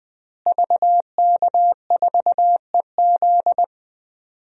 Morse code, heard as a keyed tone, about 21 wpm, 700 Hz. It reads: VK4EZ
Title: DXCC-100 Call Signs - 5 Characters - Repeated 3x - 20wpm